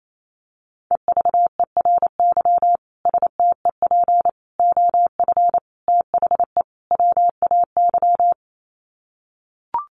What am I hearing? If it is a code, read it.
Morse code, heard as a keyed tone, about 28 wpm, 700 Hz. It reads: EVERY STEP OF THE WAY
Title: Sets of 5 Words Encore2 28wpm